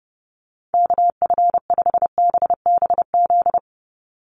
Morse code, heard as a keyed tone, about 30 wpm, 700 Hz. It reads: KF5BBZ
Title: USA Call Signs 30wpm